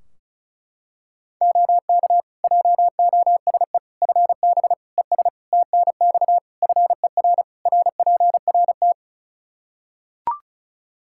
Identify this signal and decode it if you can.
Morse code, heard as a keyed tone, about 36 wpm, 700 Hz. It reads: OK JOSE FB ES TNX FER RPRT
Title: K1USN SST QSO Practice 35wpm